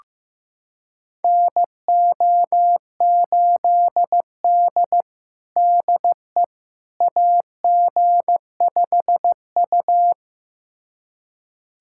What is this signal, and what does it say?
Morse code, heard as a keyed tone, about 15 wpm, 700 Hz. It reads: NO8D DE AG5U
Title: USA Callsign DE Callsign - 4 Characters - 15wpm